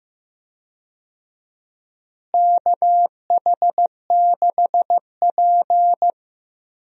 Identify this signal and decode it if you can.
Morse code, heard as a keyed tone, about 15 wpm, 700 Hz. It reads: KH6P
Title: DXCC-100 Call Signs - 4 Characters - Repeated 3x - 15wpm